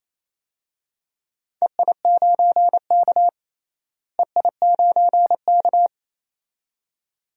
Morse code, heard as a keyed tone, about 28 wpm, 700 Hz. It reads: EI9K EI9K
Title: DXCC-100 Call Signs - 4 Characters - Repeated 2x - 28wpm